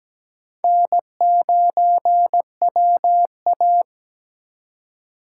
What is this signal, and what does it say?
Morse code, heard as a keyed tone, about 17 wpm, 700 Hz. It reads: N9WA
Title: USA Call Signs - 4 Characters - 4x Character Spacing - 17wpm